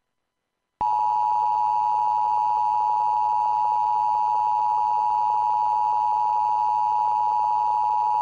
Emergency Alert System (SAME) tones (the 853 and 960 Hz attention tone). It is heard from 0.9 to 8.0 s.